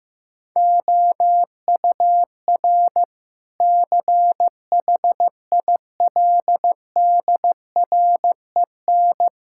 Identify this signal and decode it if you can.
Morse code, heard as a keyed tone, about 15 wpm, 700 Hz. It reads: OUR CHILDREN
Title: Sentences from Top 1000 Words - Review Part 5 - 15wpm